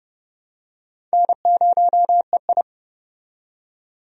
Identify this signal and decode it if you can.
Morse code, heard as a keyed tone, about 31 wpm, 700 Hz. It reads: N0EI
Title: USA Call Signs - 4 Characters - 30wpm